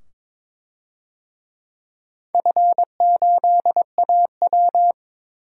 Morse code, heard as a keyed tone, about 22 wpm, 700 Hz. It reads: F8AW
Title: DXCC-100 Call Signs - 4 Characters - 22wpm